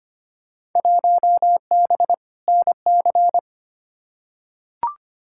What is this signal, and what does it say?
Morse code, heard as a keyed tone, about 25 wpm, 700 Hz. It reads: 1B NC
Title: Field Day Exchanges Encore - 25wpm